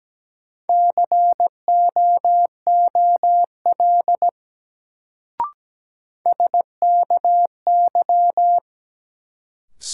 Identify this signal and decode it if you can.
Morse code, heard as a keyed tone, about 17 wpm, 700 Hz. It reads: COOL SKY
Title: Sentences from Top 1200 Words - New Word Drill Alt - 17wpm